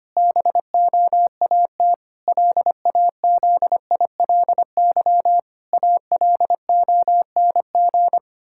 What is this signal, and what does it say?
Morse code, heard as a keyed tone, about 25 wpm, 700 Hz. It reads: BOAT LAZILY ALONG
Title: Pangram Challenge v2 - 25wpm